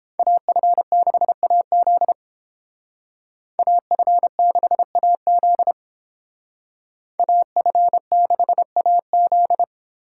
Morse code, heard as a keyed tone, about 29 wpm, 700 Hz. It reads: AF6AZ AF6AZ AF6AZ